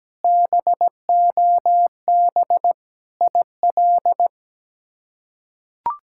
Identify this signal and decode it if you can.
Morse code, heard as a keyed tone, about 17 wpm, 700 Hz. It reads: BOB IL
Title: K1USN SST QSO Practice - 17wpm